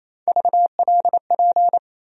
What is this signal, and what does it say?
Morse code, heard as a keyed tone, about 28 wpm, 700 Hz. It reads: VLP
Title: USA Call Sign Postfixes - 3 Letters 28wpm